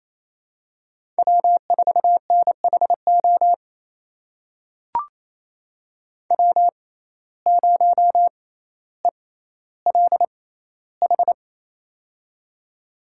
Morse code, heard as a keyed tone, about 28 wpm, spaced as for 14 wpm, 700 Hz. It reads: W4NHO W0ELH
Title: USA Call Signs - 5 Characters - 6x Character Spacing - 28wpm